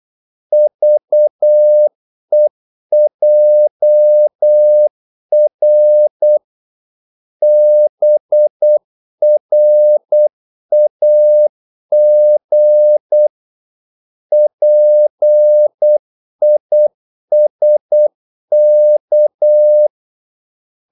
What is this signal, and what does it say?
Morse code, heard as a keyed tone, about 8 wpm, 600 Hz. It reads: VEJR BRAG PISK
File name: Korte ord DK 8wpm